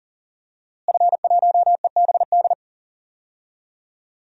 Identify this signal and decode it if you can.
Morse code, heard as a keyed tone, about 37 wpm, 700 Hz. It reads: F1EBD
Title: DXCC-100 Call Signs - 5 Characters - Repeated 3x - 40wpm